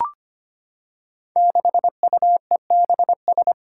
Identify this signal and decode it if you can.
Morse code, heard as a keyed tone, about 25 wpm, 700 Hz. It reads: 6UEBS
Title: Five Challenging Letter-Number Combinations 25wpm